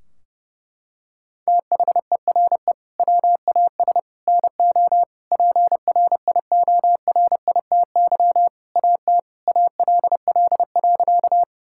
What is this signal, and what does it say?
Morse code, heard as a keyed tone, about 30 wpm, 700 Hz. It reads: THERE WAS NO PRIORITY AT ALL.
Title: Sentences from Top 1600 Words 30wpm